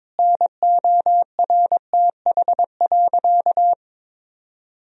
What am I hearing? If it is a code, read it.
Morse code, heard as a keyed tone, about 21 wpm, 700 Hz. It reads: NORTH.
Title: Sentences from Top 1000 Words - Review Part 1 - 22wpm